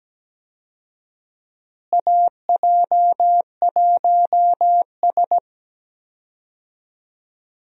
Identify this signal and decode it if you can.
Morse code, heard as a keyed tone, about 17 wpm, 700 Hz. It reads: AJ1S
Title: USA Call Signs - 4 Characters - Repeated 2x - 17wpm